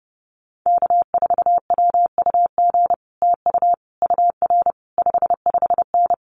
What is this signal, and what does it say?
Morse code, heard as a keyed tone, about 30 wpm, 700 Hz. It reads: K4WUG TU UR 55N
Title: Common QSO Phrases Encore 30wpm